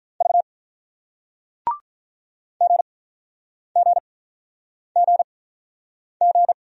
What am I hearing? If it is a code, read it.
Morse code, heard as a keyed tone, about 50 wpm, spaced as for 9 wpm, 700 Hz. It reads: V GGGG